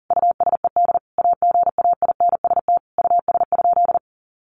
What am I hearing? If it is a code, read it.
Morse code, heard as a keyed tone, about 37 wpm, 700 Hz. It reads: USED AGAINST US?
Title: Sentences from Top 1500 Words 40wpm